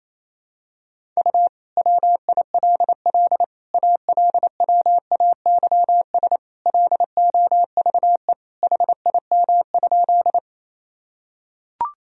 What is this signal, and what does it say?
Morse code, heard as a keyed tone, about 28 wpm, 700 Hz. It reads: U WILL ALWAYS LOVE HIM?
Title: Sentences from Top 400 Words and Abbreviations 28wpm